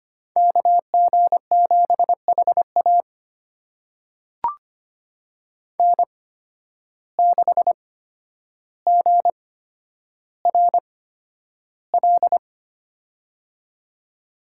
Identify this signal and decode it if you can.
Morse code, heard as a keyed tone, about 25 wpm, spaced as for 11 wpm, 700 Hz. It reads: KG7HA N6GRL
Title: USA Call Signs - 5 Characters - 8x Character Spacing - 25wpm